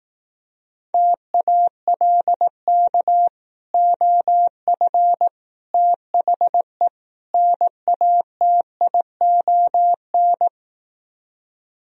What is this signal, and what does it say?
Morse code, heard as a keyed tone, about 18 wpm, 700 Hz. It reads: TALK OF THE NATION
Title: Speed-Racing - Sets of 4 Words 15wpm